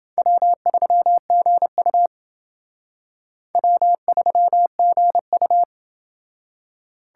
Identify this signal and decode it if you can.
Morse code, heard as a keyed tone, about 28 wpm, 700 Hz. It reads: W3GU W3GU